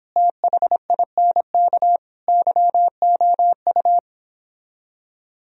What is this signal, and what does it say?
Morse code, heard as a keyed tone, about 29 wpm, 700 Hz. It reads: THINK YOU